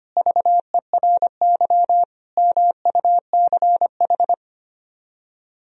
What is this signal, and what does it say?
Morse code, heard as a keyed tone, about 30 wpm, 700 Hz. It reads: VERY MUCH
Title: Speed-Racing - Sets of 3 Words 25wpm